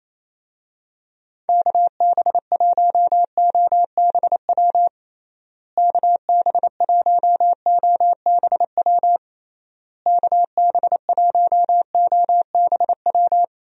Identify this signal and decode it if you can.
Morse code, heard as a keyed tone, about 28 wpm, 700 Hz. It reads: KB1OBW KB1OBW KB1OBW
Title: USA Call Signs - 6 Characters - Repeated 3x - 28wpm